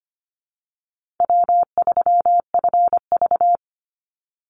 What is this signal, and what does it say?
Morse code, heard as a keyed tone, about 25 wpm, 700 Hz. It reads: W3FV
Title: Easy Call Signs - 25wpm